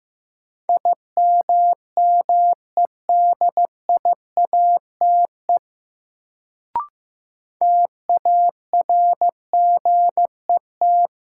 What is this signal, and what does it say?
Morse code, heard as a keyed tone, about 15 wpm, 700 Hz. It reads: IMMEDIATE TARGET
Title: Sentences from Top 1200 Words - New Word Drill - 15wpm